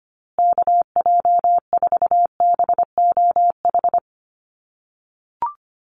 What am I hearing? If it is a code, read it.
Morse code, heard as a keyed tone, about 25 wpm, 700 Hz. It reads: KJ4BOH
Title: USA Call Signs Repeated 2x 25wpm